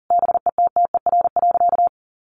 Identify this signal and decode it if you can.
Morse code, heard as a keyed tone, about 40 wpm, 700 Hz. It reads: BETTER.
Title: Easy Phrases 40wpm